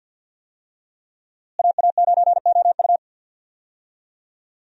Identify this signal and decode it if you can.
Morse code, heard as a keyed tone, about 52 wpm, 690 Hz. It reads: AA9OU
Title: USA Call Signs - 5 Characters - Repeated 3x - 50wpm